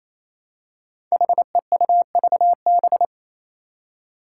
Morse code, heard as a keyed tone, about 28 wpm, 700 Hz. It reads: HEUVB
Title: Five Challenging Letter-Number Combinations 28wpm